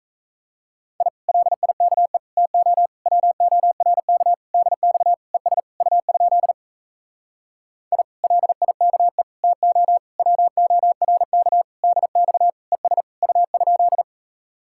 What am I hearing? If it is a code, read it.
Morse code, heard as a keyed tone, about 37 wpm, 700 Hz. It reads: I LIKE TO WORK DX ES U? I LIKE TO WORK DX ES U?